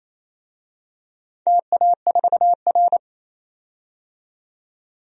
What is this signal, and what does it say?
Morse code, heard as a keyed tone, about 29 wpm, 700 Hz. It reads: TA4R
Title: DXCC-100 Call Signs - 4 Characters - 28wpm